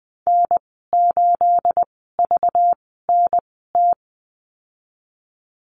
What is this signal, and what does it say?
Morse code, heard as a keyed tone, about 20 wpm, spaced as for 15 wpm, 700 Hz. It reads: N8VNT
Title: 231 Lesson 37g - Letter x - Call Signs 20wpm